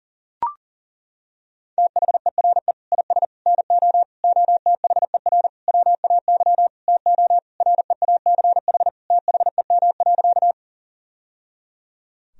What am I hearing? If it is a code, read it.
Morse code, heard as a keyed tone, about 40 wpm, 700 Hz. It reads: THERE IS NO OTHER WAY TO REACH THEM.
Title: Sentences from Top 1000 Words - Review Part 2 - 40wpm